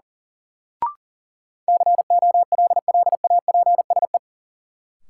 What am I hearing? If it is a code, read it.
Morse code, heard as a keyed tone, about 37 wpm, 700 Hz. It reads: COLLAPSE
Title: Sets of 5 Words Encore - New Word Drill - 40wpm